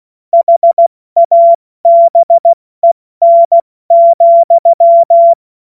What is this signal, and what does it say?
Morse code, heard as a keyed tone, about 16 wpm, 680 Hz. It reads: HABEN,
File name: Witz-30-September-2025-70-F80